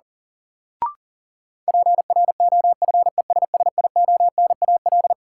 Morse code, heard as a keyed tone, about 37 wpm, 700 Hz. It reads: PROFESSIONAL
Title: Sentences from Top 1000 Words - Review Part 4 - New Word Drill - 40wpm